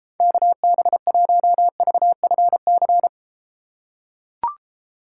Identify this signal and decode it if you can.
Morse code, heard as a keyed tone, about 32 wpm, 700 Hz. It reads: KB1VFC